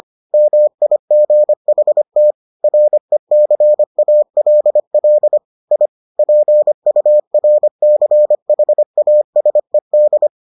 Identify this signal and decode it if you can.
Morse code, heard as a keyed tone, about 25 wpm, 600 Hz. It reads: MIGHT RECALL I PURCHASED